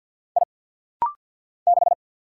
Morse code, heard as a keyed tone, about 50 wpm, 700 Hz. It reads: I 6
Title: Single Letter-Number - Rapid-Fire 50wpm